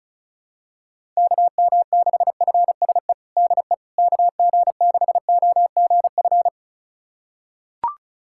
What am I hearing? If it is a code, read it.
Morse code, heard as a keyed tone, about 34 wpm, 700 Hz. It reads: KM6FSE DE KG6OGF
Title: USA Callsign DE Callsign - 6 Characters - 35wpm